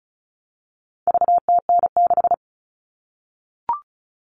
Morse code, heard as a keyed tone, about 34 wpm, 700 Hz. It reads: VTN6
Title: All Cut Numbers - 4-Digit 35wpm